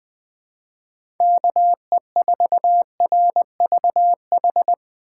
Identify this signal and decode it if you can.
Morse code, heard as a keyed tone, about 20 wpm, 700 Hz. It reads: KE4RVH
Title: Speed Racing - USA 2x3 Call Signs 17wpm